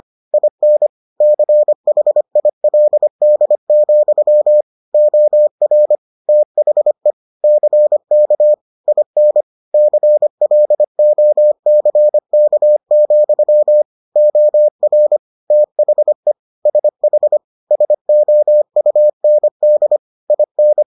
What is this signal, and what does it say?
Morse code, heard as a keyed tone, about 25 wpm, 600 Hz. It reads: IN CHILD, OR THE CK IN CLOCK, OR THE SH SOUND IN